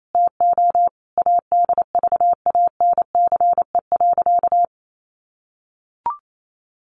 Morse code, heard as a keyed tone, about 28 wpm, 700 Hz. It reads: TO ADVANCE.
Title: Sentences from Top 1400 Words 28wpm